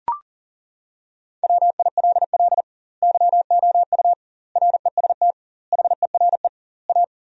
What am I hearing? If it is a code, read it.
Morse code, heard as a keyed tone, about 40 wpm, 700 Hz. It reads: WILL YOU REST HERE A
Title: 2x Word-Spacing - Sentences from Top 500 Words - 40wpm